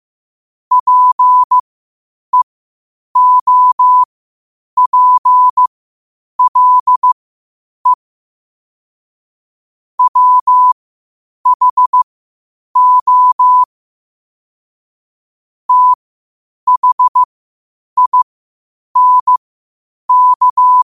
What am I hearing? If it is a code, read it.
Morse code, heard as a keyed tone, about 15 wpm, spaced as for 8 wpm, 1000 Hz. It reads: PEOPLE WHO THINK
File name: qotdfiles / 8wpm